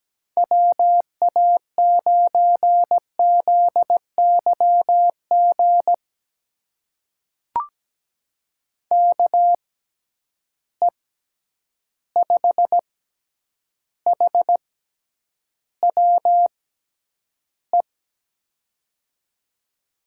Morse code, heard as a keyed tone, about 17 wpm, spaced as for 9 wpm, 700 Hz. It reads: WA9ZYG KE5HWE
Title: USA Call Signs - 6 Characters - 6x Character Spacing - 17wpm